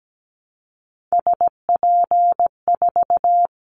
Morse code, heard as a keyed tone, about 17 wpm, 700 Hz. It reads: SP4
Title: Callsign Prefixes for Top 25 DXCC - Repeated 3x - 17wpm